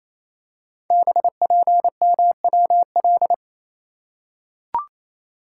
Morse code, heard as a keyed tone, about 28 wpm, 700 Hz. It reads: BPMWL
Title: Code Groups - 5 Letters - 28wpm